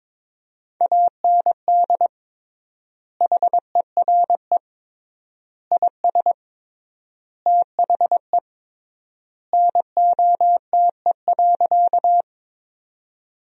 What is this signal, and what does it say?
Morse code, heard as a keyed tone, about 22 wpm, 700 Hz. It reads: AND HERE IS THE NOTE.
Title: 3x Word-Spacing - Sentences from Top 500 Words - 22wpm